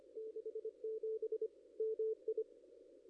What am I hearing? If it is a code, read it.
Morse code, heard as a keyed tone, about 24 wpm, 440 Hz. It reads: 67 MI
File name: contests / ss-cw-k5tr-op-n5rz -1700